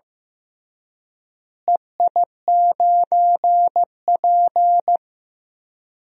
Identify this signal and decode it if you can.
Morse code, heard as a keyed tone, about 15 wpm, 700 Hz. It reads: EI9P
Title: DXCC-100 Call Signs - 4 Characters - Repeated 3x - 15wpm